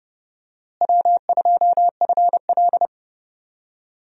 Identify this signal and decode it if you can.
Morse code, heard as a keyed tone, about 32 wpm, 700 Hz. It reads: W2FL
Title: USA Call Signs - 4 Characters - Repeated 2x - 30wpm